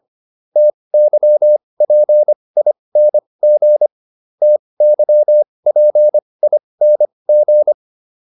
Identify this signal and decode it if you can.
Morse code, heard as a keyed tone, about 25 wpm, spaced as for 20 wpm, 600 Hz. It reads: TYPING TYPING